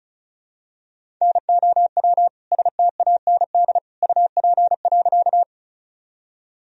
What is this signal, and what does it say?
Morse code, heard as a keyed tone, about 36 wpm, 700 Hz. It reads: NOW STAND UP.
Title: Sentences from Top 1000 Words - Review Part 1 - 35wpm